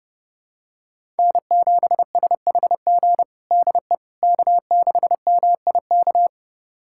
Morse code, heard as a keyed tone, about 29 wpm, 700 Hz. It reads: N7SHG DE K6MIK
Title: USA Callsign DE Callsign - 5 Characters - 30wpm